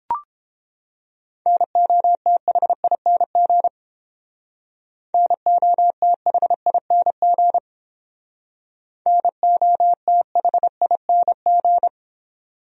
Morse code, heard as a keyed tone, about 30 wpm, 700 Hz. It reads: NOTHING NOTHING NOTHING